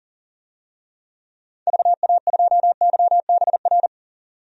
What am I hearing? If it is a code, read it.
Morse code, heard as a keyed tone, about 40 wpm, 700 Hz. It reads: VA2YBR
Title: DXCC-100 Call Signs - 6 Characters - 40wpm